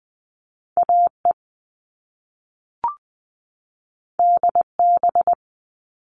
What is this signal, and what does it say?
Morse code, heard as a keyed tone, about 20 wpm, 700 Hz. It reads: AE DB
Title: All Cut Numbers - 2-Digit 20wpm